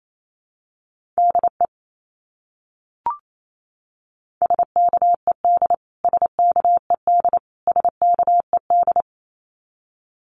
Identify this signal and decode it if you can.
Morse code, heard as a keyed tone, about 28 wpm, 700 Hz. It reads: DE SKED SKED SKED
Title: Abbreviations Repeated 3x - 28wpm